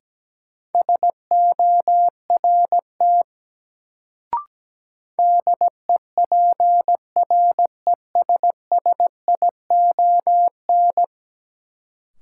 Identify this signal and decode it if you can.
Morse code, heard as a keyed tone, about 17 wpm, 700 Hz. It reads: SORT DEPRESSION
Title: Sentences from Top 1300 Words - New Word Drill Alt - 17wpm